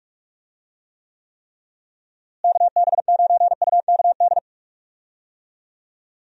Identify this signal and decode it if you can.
Morse code, heard as a keyed tone, about 46 wpm, 700 Hz. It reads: KB9UKD
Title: USA Call Signs - 6 Characters - Repeated 3x - 45wpm